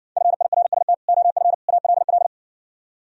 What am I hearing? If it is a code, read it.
Morse code, heard as a keyed tone, about 67 wpm, 700 Hz. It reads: FIRST OF ALL